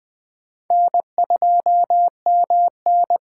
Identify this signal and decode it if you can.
Morse code, heard as a keyed tone, about 21 wpm, 700 Hz. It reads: N2MN
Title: Speed Racing - USA Call Signs 17wpm